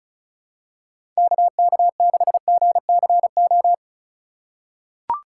Morse code, heard as a keyed tone, about 36 wpm, 700 Hz. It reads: KK6GCO
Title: USA Call Signs - 6x Character Spacing - 35wpm